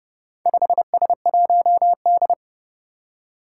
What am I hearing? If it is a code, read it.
Morse code, heard as a keyed tone, about 30 wpm, 700 Hz. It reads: 5S1D
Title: Four Challenging Letter-Number Combinations 30wpm